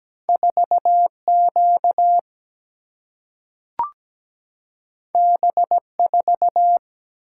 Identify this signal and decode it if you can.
Morse code, heard as a keyed tone, about 17 wpm, 700 Hz. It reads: 4Q B4
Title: Single-Letter Single-Number 17wpm